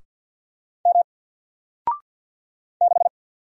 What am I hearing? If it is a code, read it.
Morse code, heard as a keyed tone, about 50 wpm, 700 Hz. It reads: M 6
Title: Single Character (Letter:Number:Punctuation:Prosign) Alt 50wpm